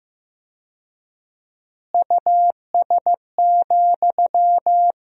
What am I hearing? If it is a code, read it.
Morse code, heard as a keyed tone, about 15 wpm, 700 Hz. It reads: US,
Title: 4x Word-Spacing - Sentences from Top 400 Words - 15wpm